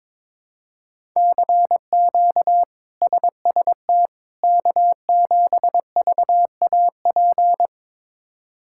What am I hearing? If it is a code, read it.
Morse code, heard as a keyed tone, about 22 wpm, 700 Hz. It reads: CQ SST K7VAP
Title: K1USN SST QSO Practice - 22wpm